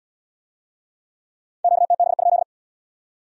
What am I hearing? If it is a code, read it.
Morse code, heard as a keyed tone, about 75 wpm, 700 Hz. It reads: QTB?